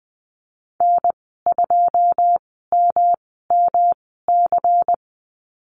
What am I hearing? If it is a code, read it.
Morse code, heard as a keyed tone, about 20 wpm, spaced as for 14 wpm, 700 Hz. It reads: N2MMC
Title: 122 Lesson 20g - Number 3 - Call Signs 20wpm